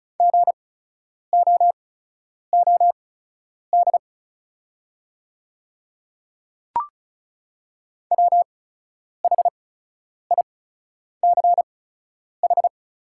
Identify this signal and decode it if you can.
Morse code, heard as a keyed tone, about 34 wpm, spaced as for 10 wpm, 700 Hz. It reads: GOOD WHICH
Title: 8x Character Spacing - Top 300 Words - 35wpm